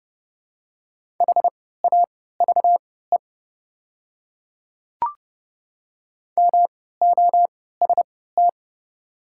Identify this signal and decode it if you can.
Morse code, heard as a keyed tone, about 30 wpm, spaced as for 17 wpm, 700 Hz. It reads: HAVE MOST
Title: 3x Character Spacing - Top 100 Words - 30wpm